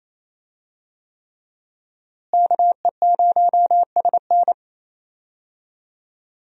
Morse code, heard as a keyed tone, about 28 wpm, 700 Hz. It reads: KE0SN
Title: USA Call Signs - 5 Characters - Repeated 3x - 28wpm